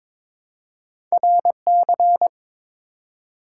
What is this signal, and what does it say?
Morse code, heard as a keyed tone, about 23 wpm, 700 Hz. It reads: RC
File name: practice / 22wpm / Two Letter-Number Combinations - 22wpm